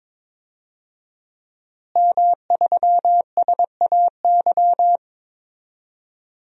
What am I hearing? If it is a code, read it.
Morse code, heard as a keyed tone, about 21 wpm, 700 Hz. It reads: M3SAY
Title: DXCC-100 Call Signs - 5 Characters - Repeated 3x - 22wpm